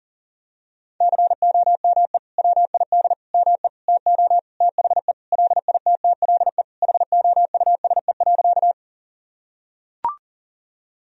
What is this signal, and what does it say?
Morse code, heard as a keyed tone, about 40 wpm, 700 Hz. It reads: COME WID ME TO THE LITTLE HOUSE.
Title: Sentences from Top 200 Words and Abbreviations 40wpm